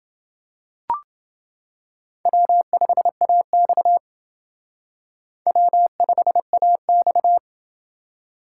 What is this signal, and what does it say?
Morse code, heard as a keyed tone, about 29 wpm, 700 Hz. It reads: W5AX W5AX